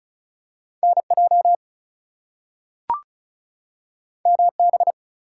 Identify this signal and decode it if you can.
Morse code, heard as a keyed tone, about 36 wpm, 700 Hz. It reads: NJ MB
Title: US States and Canadian Provinces 35wpm